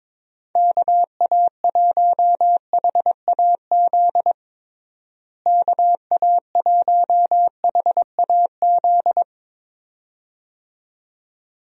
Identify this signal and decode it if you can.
Morse code, heard as a keyed tone, about 22 wpm, 700 Hz. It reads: KA1HAZ KA1HAZ
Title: USA Call Signs - 6 Characters - Repeated 2x - 22wpm